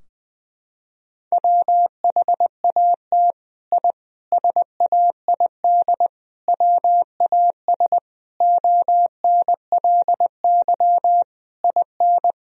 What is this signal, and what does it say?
Morse code, heard as a keyed tone, about 20 wpm, 700 Hz. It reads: WHAT I SAID WAS ONLY IN
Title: Sentences from Top 1000 Words - Review Part 4 - 20wpm